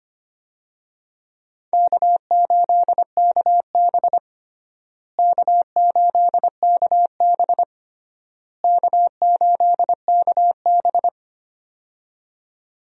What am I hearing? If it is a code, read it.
Morse code, heard as a keyed tone, about 25 wpm, 700 Hz. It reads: K8KB K8KB K8KB
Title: USA Call Signs - 4 Characters - Repeated 3x - 25wpm